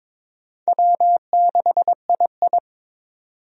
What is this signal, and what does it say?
Morse code, heard as a keyed tone, about 22 wpm, 700 Hz. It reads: W6II
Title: USA Call Signs - 4 Characters - 22wpm